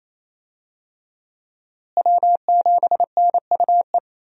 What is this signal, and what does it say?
Morse code, heard as a keyed tone, about 28 wpm, 700 Hz. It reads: W7NUE
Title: USA Call Signs - 5 Characters - Repeated 2x - 28wpm